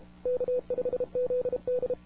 Morse code, heard as a keyed tone, about 32 wpm, 500 Hz. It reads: K5ZD
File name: wwcw08 / k5zd wwcw08 1700z